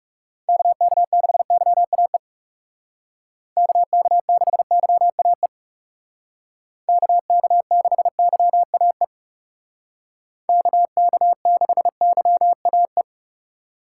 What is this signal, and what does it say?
Morse code, heard as a keyed tone, about 38 wpm, 700 Hz. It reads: KK6YAE KK6YAE KK6YAE KK6YAE
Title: Speed Racing - USA 2x3 Call Signs 30wpm